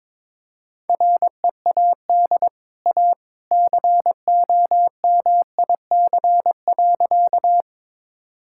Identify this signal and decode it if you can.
Morse code, heard as a keyed tone, about 22 wpm, 700 Hz. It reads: READ A COMIC.
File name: practice / 22wpm / Easy Sentences with Hints - 22wpm